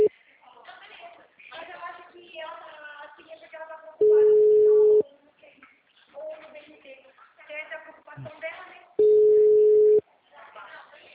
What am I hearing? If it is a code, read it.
Ringback tone (Europe, 425 Hz), with speech.